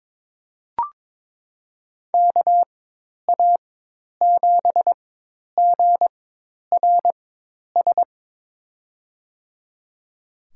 Morse code, heard as a keyed tone, about 21 wpm, spaced as for 10 wpm, 700 Hz. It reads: KA7GRS
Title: USA Call Signs - 4x Character Spacing - 22wpm